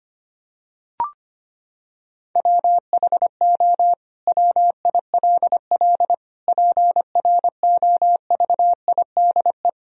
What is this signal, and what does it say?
Morse code, heard as a keyed tone, about 25 wpm, 700 Hz. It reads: WHO WILL PROVIDE
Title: Sentences from Top 1000 Words - Review Part 3 - 25wpm